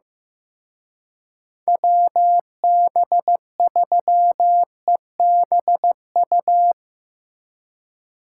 Morse code, heard as a keyed tone, about 17 wpm, 700 Hz. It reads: WB3EBU
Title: Speed Racing - USA 2x3 Call Signs 15wpm